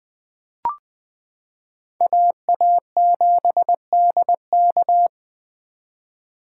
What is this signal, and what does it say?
Morse code, heard as a keyed tone, about 20 wpm, 700 Hz. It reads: AA7DK
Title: 2023 SOTA Chaser Call Signs - 20wpm